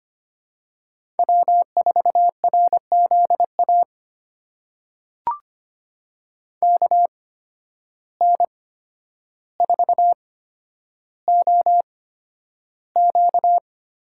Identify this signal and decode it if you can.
Morse code, heard as a keyed tone, about 25 wpm, spaced as for 11 wpm, 700 Hz. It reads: W4RZA KN4OQ
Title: USA Call Signs - 5 Characters - 8x Character Spacing - 25wpm